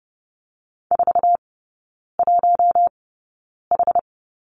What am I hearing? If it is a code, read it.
Morse code, heard as a keyed tone, about 30 wpm, spaced as for 9 wpm, 700 Hz. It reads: 41H
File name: 081 Lesson 14e - Number 2 - Three Characters 30wpm